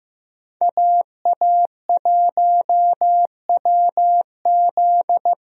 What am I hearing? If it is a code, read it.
Morse code, heard as a keyed tone, about 15 wpm, 700 Hz. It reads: AA1WZ
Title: USA Call Signs - 5 Characters - Repeated 2x - 15wpm